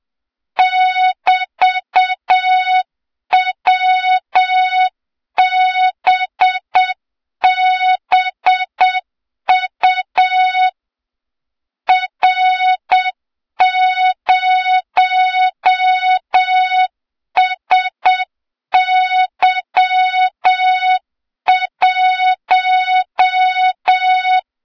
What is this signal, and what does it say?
Morse code, heard as a keyed tone, about 7 wpm, 3700 Hz, with noise. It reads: <BT>WBBU R0SY1
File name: Morze-HA2MN / hang / 23 GYAKORLAT